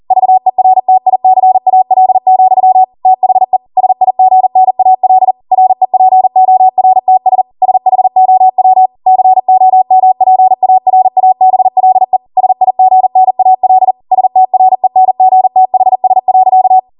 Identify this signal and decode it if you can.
Morse code, heard as a keyed tone, about 40 wpm, 750 Hz. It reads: VERTICAL, THE SIGNAL REPORTS SHOW COMPARABLE SIGNAL STRENGTHS.